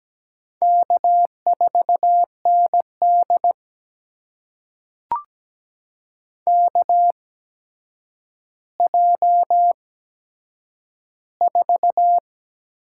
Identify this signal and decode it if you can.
Morse code, heard as a keyed tone, about 17 wpm, spaced as for 8 wpm, 700 Hz. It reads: K4ND KJ4
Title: USA Call Signs - 8x Character Spacing - 17wpm